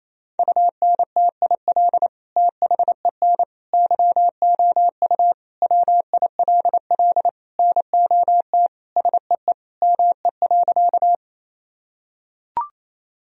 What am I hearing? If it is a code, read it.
Morse code, heard as a keyed tone, about 28 wpm, 700 Hz. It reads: UNTIL THEN YOU WILL NOT SEE ME.
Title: Sentences from Top 1000 Words - Review Part 1 - 28wpm